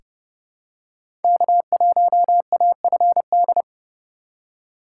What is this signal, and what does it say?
Morse code, heard as a keyed tone, about 29 wpm, 700 Hz. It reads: K1AFD
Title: USA Call Signs - 4x Character Spacing - 30wpm